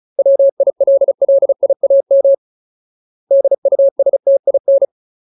Morse code, heard as a keyed tone, about 36 wpm, 550 Hz. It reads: WILLIAM DUSTIN
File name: Ultimate Word List / 35WPM Ultimate